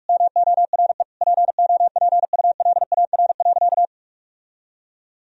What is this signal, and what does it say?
Morse code, heard as a keyed tone, about 46 wpm, 700 Hz. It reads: MORE POPULAR.
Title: Sentences from Top 1000 Words - Review Part 3 - 45wpm